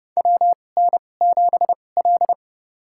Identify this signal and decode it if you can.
Morse code, heard as a keyed tone, about 30 wpm, spaced as for 22 wpm, 700 Hz. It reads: WN7L
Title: USA Call Signs - 4 Characters - 2x Character Spacing - 30wpm